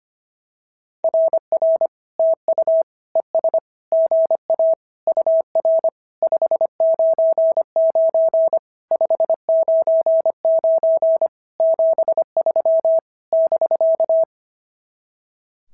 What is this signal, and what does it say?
Morse code, heard as a keyed tone, about 24 wpm, 650 Hz. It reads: RR TU ES GA UR 599 599 73 <BK>
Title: SOTA QSO Practice 25wpm